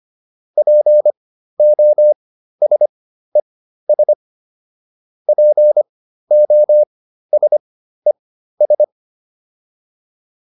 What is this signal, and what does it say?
Morse code, heard as a keyed tone, about 25 wpm, spaced as for 13 wpm, 600 Hz. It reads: POSES POSES